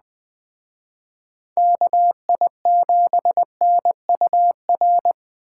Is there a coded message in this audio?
Yes: Morse code from 1.6 to 5.1 s, heard as a keyed tone, about 20 wpm, 700 Hz. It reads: KI7NUR